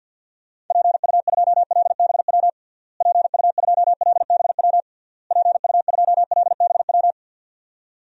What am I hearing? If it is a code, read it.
Morse code, heard as a keyed tone, about 52 wpm, 700 Hz. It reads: PU2LBW PU2LBW PU2LBW
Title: DXCC-100 Call Signs - 6 Characters - Repeated 3x - 50wpm